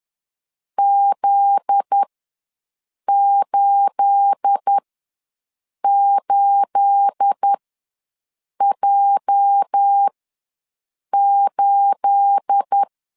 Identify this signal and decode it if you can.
Morse code, heard as a keyed tone, about 10 wpm, spaced as for 6 wpm, 800 Hz. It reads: Z88J8
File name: LEZIONI / 9 28 ws 10wpm 4124